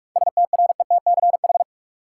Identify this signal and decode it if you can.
Morse code, heard as a keyed tone, about 47 wpm, 700 Hz. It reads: STRETCH
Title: Seven Letter Words - 45wpm